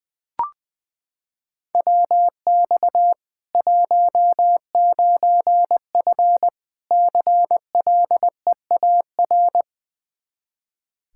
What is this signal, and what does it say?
Morse code, heard as a keyed tone, about 21 wpm, 700 Hz. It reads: WX 19F CLEAR
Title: Practice QSO - Exchange 2 - Info - 20wpm